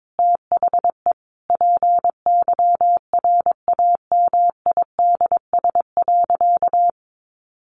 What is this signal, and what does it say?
Morse code, heard as a keyed tone, about 22 wpm, 700 Hz. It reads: THE PYRAMIDS.
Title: A Sound Of Thunder 22wpm